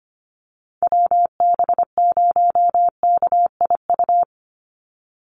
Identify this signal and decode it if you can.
Morse code, heard as a keyed tone, about 25 wpm, 700 Hz. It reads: WB0KIU
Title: NA SOTA Chaser Callsigns 25wpm